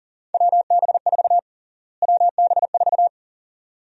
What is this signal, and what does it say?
Morse code, heard as a keyed tone, about 40 wpm, 700 Hz. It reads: WB4 WB4
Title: USA Call Sign Prefixes - Repeated 2x - 40wpm